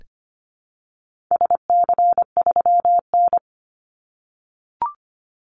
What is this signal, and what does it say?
Morse code, heard as a keyed tone, about 25 wpm, 700 Hz. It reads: SC3N
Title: Call Signs - 25wpm